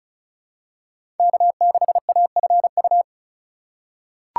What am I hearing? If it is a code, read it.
Morse code, heard as a keyed tone, about 36 wpm, 700 Hz. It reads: K6AFU
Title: USA Call Signs - 2x Character Spacing - 35wpm